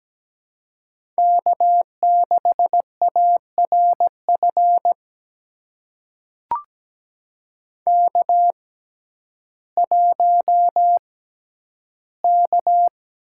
Morse code, heard as a keyed tone, about 17 wpm, spaced as for 10 wpm, 700 Hz. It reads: K6ARF K1K
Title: USA Call Signs - 5 Characters - 6x Character Spacing - 17wpm